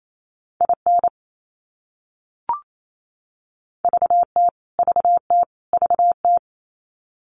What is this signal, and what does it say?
Morse code, heard as a keyed tone, about 28 wpm, 700 Hz. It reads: IN VT VT VT
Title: US States Repeated 3x 28wpm